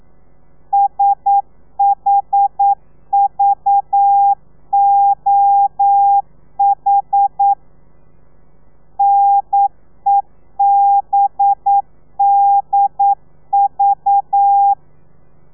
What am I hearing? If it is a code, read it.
Morse code, heard as a keyed tone, about 9 wpm, 800 Hz. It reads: SHVOH NEBDV
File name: telegrafia / i2tfj / Lez4 40